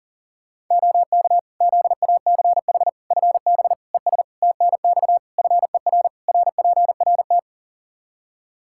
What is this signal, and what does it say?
Morse code, heard as a keyed tone, about 40 wpm, 700 Hz. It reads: OK ZACH FB ES TNX FER RPRT
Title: K1USN SST QSO Practice 40wpm